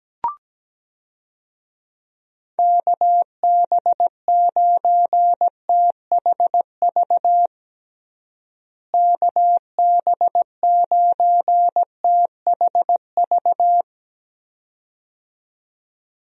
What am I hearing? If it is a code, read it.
Morse code, heard as a keyed tone, about 17 wpm, 700 Hz. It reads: KB9THV KB9THV
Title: USA Call Signs - 6 Characters - Repeated 2x - 17wpm